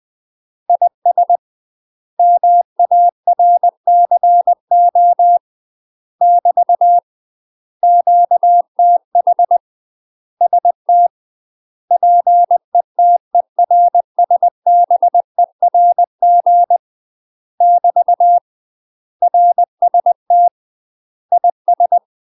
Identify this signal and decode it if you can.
Morse code, heard as a keyed tone, about 20 wpm, 700 Hz. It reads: IS MARCO <BT> QTH ST PETERSBERG <BT> RST IS
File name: UK MASTER QSOs TRAINING / 20WPM / UK MASTER QSOs 0044